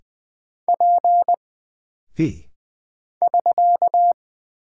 Morse code, heard as a keyed tone, about 20 wpm, 700 Hz. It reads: P <SK>
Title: Single Character (Letter:Number:Punctuation:Prosign) - Mind-Melt - 20wpm